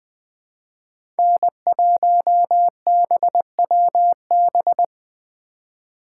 Morse code, heard as a keyed tone, about 20 wpm, 700 Hz. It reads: N1BWB
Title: USA Call Signs - 4x Character Spacing - 20wpm